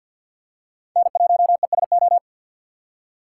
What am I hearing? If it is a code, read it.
Morse code, heard as a keyed tone, about 46 wpm, 700 Hz. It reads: N1ESO
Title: USA Call Signs - 5 Characters - 4x Character Spacing - 50wpm